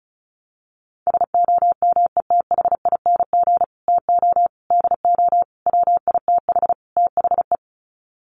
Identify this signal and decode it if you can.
Morse code, heard as a keyed tone, about 34 wpm, 700 Hz. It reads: SOMETHING TO DO WITH THE
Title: Sets of 5 Words Encore 35wpm